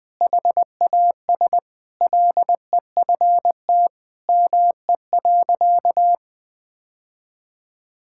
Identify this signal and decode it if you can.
Morse code, heard as a keyed tone, about 21 wpm, 700 Hz. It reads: HAS LEFT ME.
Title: Sentences from Top 1600 Words 20wpm